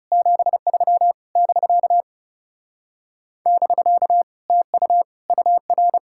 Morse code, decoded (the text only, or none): 73 <BK> <BK> TU UR